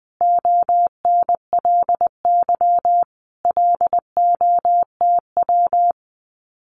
Morse code, heard as a keyed tone, about 20 wpm, 700 Hz. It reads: ONLY LOTW
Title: Common QSO Phrases Encore 20wpm